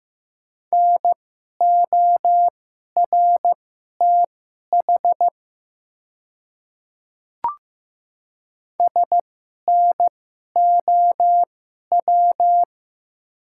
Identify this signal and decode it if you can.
Morse code, heard as a keyed tone, about 15 wpm, spaced as for 11 wpm, 700 Hz. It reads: NORTH SNOW
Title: 2x Character Spacing - Top 500 Words - 15wpm